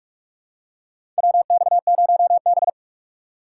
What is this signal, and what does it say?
Morse code, heard as a keyed tone, about 46 wpm, 690 Hz. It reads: WX0B
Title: USA Call Signs - 4 Characters - 3x Character Spacing - 45wpm